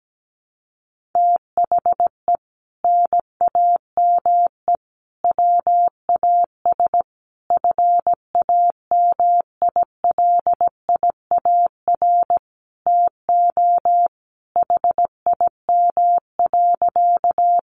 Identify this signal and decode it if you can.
Morse code, heard as a keyed tone, about 17 wpm, 700 Hz. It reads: THE NAME WAS FAMILIAR TO HIM.
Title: Sentences from Top 1000 Words 17wpm